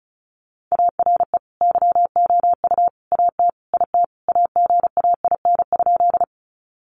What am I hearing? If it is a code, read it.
Morse code, heard as a keyed tone, about 34 wpm, 700 Hz. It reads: ARE YOU AT IT AGAIN?
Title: Sentences from Top 200 Words 35wpm